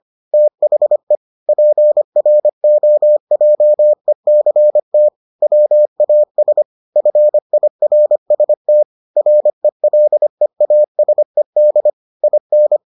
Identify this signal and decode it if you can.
Morse code, heard as a keyed tone, about 25 wpm, 600 Hz. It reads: THE PROJECT WAS FIRST RELEASED IN